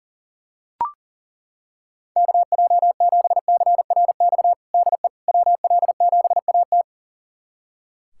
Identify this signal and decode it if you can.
Morse code, heard as a keyed tone, about 40 wpm, 700 Hz. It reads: KJ7CRX DE WL7AT
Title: USA Callsign DE Callsign 40wpm